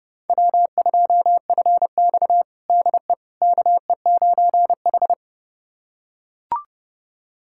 Morse code, heard as a keyed tone, about 30 wpm, 700 Hz. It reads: W2FX DE KE9H
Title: USA Callsign DE Callsign - 4 Characters - 30wpm